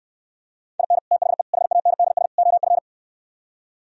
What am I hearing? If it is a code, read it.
Morse code, heard as a keyed tone, about 74 wpm, 700 Hz. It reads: IN THE HANDS OF